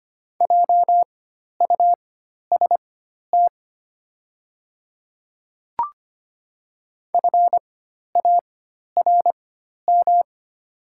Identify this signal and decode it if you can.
Morse code, heard as a keyed tone, about 25 wpm, spaced as for 12 wpm, 700 Hz. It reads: JUST FARM
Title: 4x Character Spacing - Top 500 Words - 25wpm